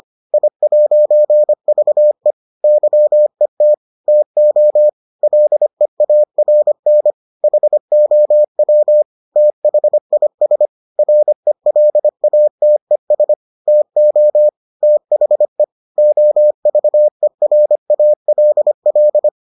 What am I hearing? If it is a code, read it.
Morse code, heard as a keyed tone, about 25 wpm, 600 Hz. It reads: I'VE YET TO LEARN HOW THIS RELATES TO THE OVERALL